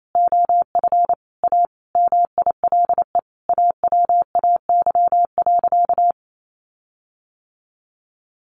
Morse code, heard as a keyed tone, about 28 wpm, 700 Hz. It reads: OF A MILE AWAY.
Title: Sentences from Top 1300 Words 28wpm